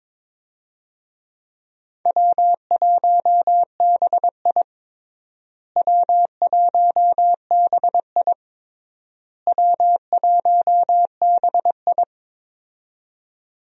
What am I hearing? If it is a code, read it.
Morse code, heard as a keyed tone, about 22 wpm, 700 Hz. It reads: W1BI W1BI W1BI
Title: USA Call Signs - 4 Characters - Repeated 3x - 22wpm